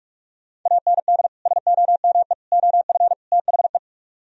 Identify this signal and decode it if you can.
Morse code, heard as a keyed tone, about 44 wpm, 700 Hz. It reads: AND SOME OF THE